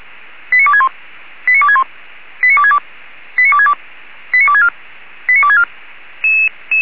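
Five-tone selective call (ZVEI), heard as quick runs of five tones, tones of about 70 ms. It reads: 89261 89261 89262 89262 89264 89264
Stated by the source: ZVEI five-tone selective call received on SDR, address 89411.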